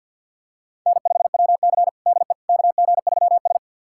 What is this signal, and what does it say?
Morse code, heard as a keyed tone, about 53 wpm, 700 Hz. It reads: N5WC DE KG3S
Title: USA Callsign DE Callsign - 4 Characters - 50wpm